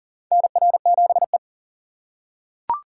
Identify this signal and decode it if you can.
Morse code, heard as a keyed tone, about 40 wpm, 700 Hz. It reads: NR7E
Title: USA Call Signs - 4 Characters - 4x Character Spacing - 40wpm